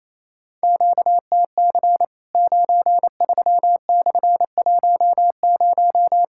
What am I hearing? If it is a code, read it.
Morse code, heard as a keyed tone, about 28 wpm, 700 Hz. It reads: QTC 93/10
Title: Worked All Europe DX Contest - QTC Exchange - 28wpm